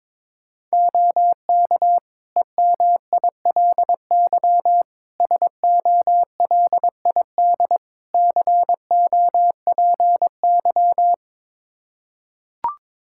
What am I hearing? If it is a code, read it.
Morse code, heard as a keyed tone, about 22 wpm, 700 Hz. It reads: OK EMILY SOLID COPY
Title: Practice QSO - Exchange 3 - More Info - 22wpm